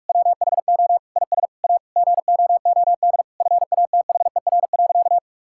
Morse code, heard as a keyed tone, about 46 wpm, 700 Hz. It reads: WHO IS A GOOD FATHER.
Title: Sentences from Top 1000 Words - Review Part 5 - 45wpm